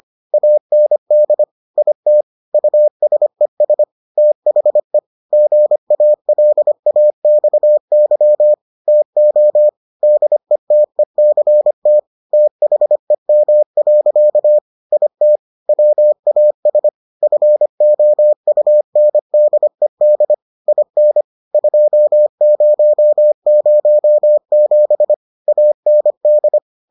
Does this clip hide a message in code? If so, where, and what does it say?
Yes: Morse code from 0.3 to 26.6 s, heard as a keyed tone, about 25 wpm, 600 Hz. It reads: AND IT USES THE GALAXY TO DETECT THEM. IT WAS FOUNDED IN 2007 AND